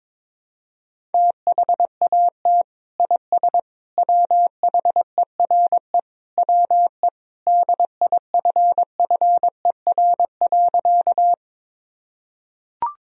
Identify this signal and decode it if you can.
Morse code, heard as a keyed tone, about 22 wpm, 700 Hz. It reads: THAT IS WHERE WE DIFFER.
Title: Sentences from Top 1000 Words - Review Part 1 - 22wpm